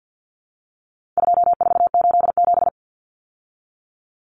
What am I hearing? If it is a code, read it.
Morse code, heard as a keyed tone, about 50 wpm, 700 Hz. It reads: 2487
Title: Numbers - 4-Digit 50wpm